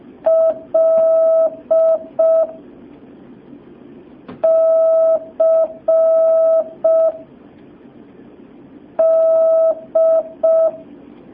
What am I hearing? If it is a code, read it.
Morse code, heard as a keyed tone, about 5 wpm, 650 Hz. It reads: L C D
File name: jh8cbh / 2019cw